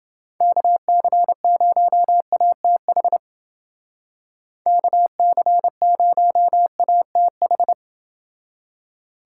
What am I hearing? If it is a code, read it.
Morse code, heard as a keyed tone, about 29 wpm, 700 Hz. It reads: KC0ATH KC0ATH